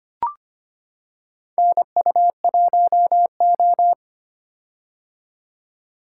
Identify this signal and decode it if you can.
Morse code, heard as a keyed tone, about 25 wpm, 700 Hz. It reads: NU1O
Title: K1USN SST QSO Practice - 25wpm